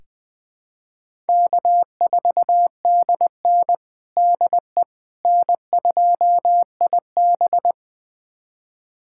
Morse code, heard as a keyed tone, about 20 wpm, 700 Hz. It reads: K4DN DE N2IB
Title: USA Callsign DE Callsign - 4 Characters - 20wpm